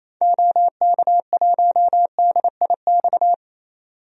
Morse code, heard as a keyed tone, about 28 wpm, 700 Hz. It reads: OK1DIX
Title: DXCC-100 Call Signs - 6 Characters - 28wpm